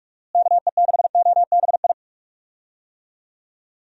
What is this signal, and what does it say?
Morse code, heard as a keyed tone, about 44 wpm, 700 Hz. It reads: KE6OBI
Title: USA Call Signs - 6 Characters - Repeated 3x - 45wpm